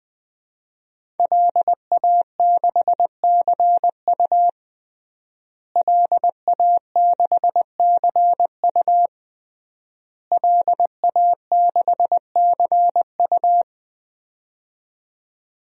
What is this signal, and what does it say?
Morse code, heard as a keyed tone, about 20 wpm, 700 Hz. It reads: LA6CU LA6CU LA6CU
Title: DXCC-100 Call Signs - 5 Characters - Repeated 3x - 20wpm